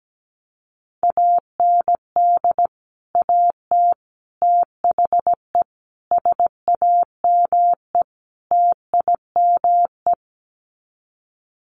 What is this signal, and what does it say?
Morse code, heard as a keyed tone, about 17 wpm, 700 Hz. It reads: AND AT THE SAME TIME
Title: Sets of 5 Words 17wpm